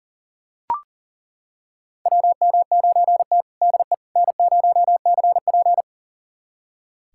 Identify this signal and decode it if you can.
Morse code, heard as a keyed tone, about 40 wpm, 700 Hz. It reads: WM9T DE N0CP
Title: USA Callsign DE Callsign - 4 Characters - 40wpm